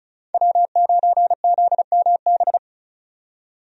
Morse code, heard as a keyed tone, about 36 wpm, 700 Hz. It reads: W9ZMB
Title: USA Call Signs - 5 Characters - 35wpm